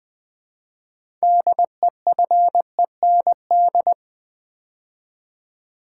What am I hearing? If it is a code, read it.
Morse code, heard as a keyed tone, about 20 wpm, 700 Hz. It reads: DEFEND
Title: 6 Letter Words - First 3 Letters Given - 20wpm